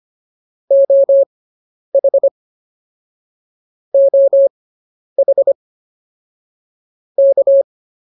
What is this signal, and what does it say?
Morse code, heard as a keyed tone, about 24 wpm, spaced as for 10 wpm, 550 Hz. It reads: OH OH K